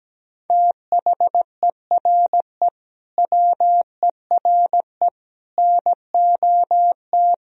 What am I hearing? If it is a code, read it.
Morse code, heard as a keyed tone, about 17 wpm, 700 Hz. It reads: THERE WERE NOT
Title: Sentences from Top 1000 Words - Review Part 1 - 17wpm